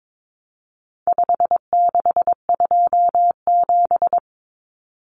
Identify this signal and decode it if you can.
Morse code, heard as a keyed tone, about 22 wpm, 700 Hz. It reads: 5627
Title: Common Cut Numbers - 4-Digit 22wpm